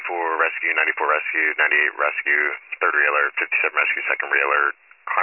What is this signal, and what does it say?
Emergency Alert System (SAME) tones, heard as harsh buzzing bursts, with speech. It is heard from 0.6 to 4.4 s.